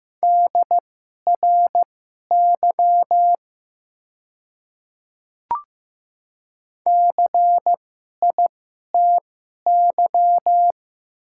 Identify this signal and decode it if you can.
Morse code, heard as a keyed tone, about 15 wpm, spaced as for 11 wpm, 700 Hz. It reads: DRY CITY
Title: 2x Character Spacing - Top 500 Words - 15wpm